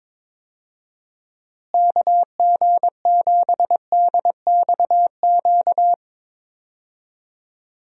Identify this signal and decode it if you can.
Morse code, heard as a keyed tone, about 22 wpm, 700 Hz. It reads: KG7DXQ
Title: USA Call Signs - 6 Characters - Repeated 2x - 22wpm